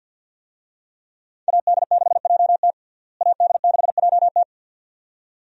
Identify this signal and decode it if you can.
Morse code, heard as a keyed tone, about 51 wpm, 700 Hz. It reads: AD6JT AD6JT
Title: USA Call Signs - 5 Characters - Repeated 2x - 50wpm